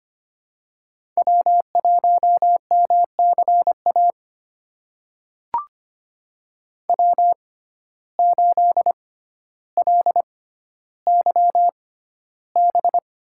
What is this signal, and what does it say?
Morse code, heard as a keyed tone, about 25 wpm, spaced as for 13 wpm, 700 Hz. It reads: W1MCA W8LYB
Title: USA Call Signs - 6x Character Spacing - 25wpm